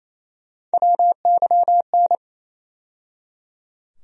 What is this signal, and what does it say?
Morse code, heard as a keyed tone, about 28 wpm, 700 Hz. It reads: WYN
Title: USA Call Sign Postfixes - 3 Letters 28wpm